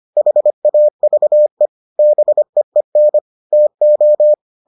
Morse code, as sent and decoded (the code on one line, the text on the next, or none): .... .- ...- . / -... . -. / - ---
HAVE BEEN TO